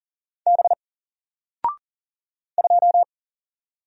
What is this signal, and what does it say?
Morse code, heard as a keyed tone, about 40 wpm, 700 Hz. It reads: B 2
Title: Single Character (Letter:Number:Punctuation:Prosign) Alt 40wpm